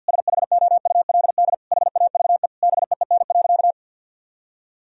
Morse code, heard as a keyed tone, about 51 wpm, 700 Hz. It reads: SHOULD HAVE BEEN.
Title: Sentences from Top 1000 Words - Review Part 4 - 50wpm